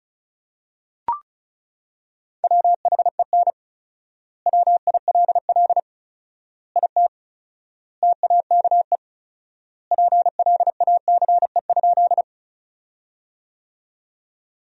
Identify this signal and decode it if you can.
Morse code, heard as a keyed tone, about 36 wpm, 700 Hz. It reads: WHEN WILL IT TAKE PLACE?
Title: 4x Word-Spacing - Sentences from Top 200 Words - 35wpm